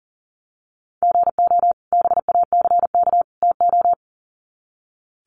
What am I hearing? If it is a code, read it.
Morse code, heard as a keyed tone, about 40 wpm, 700 Hz. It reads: GO BACK TO
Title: Sets of 3 Words 40wpm